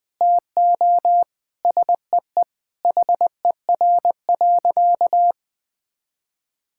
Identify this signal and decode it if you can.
Morse code, heard as a keyed tone, about 20 wpm, 700 Hz. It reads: TO SEE HER.
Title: Sentences from Top 1000 Words - Review Part 5 - 20wpm